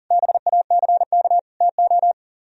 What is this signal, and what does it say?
Morse code, heard as a keyed tone, about 50 wpm, 700 Hz. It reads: BACK TO
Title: Speed-Racing - Sets of 3 Words 40wpm